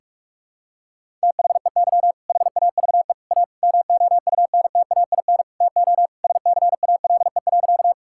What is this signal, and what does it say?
Morse code, heard as a keyed tone, about 41 wpm, 700 Hz. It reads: THEY HAVE A MOUNTAIN TO SCALE.
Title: Sentences from Top 1000 Words - Review Part 5 - 45wpm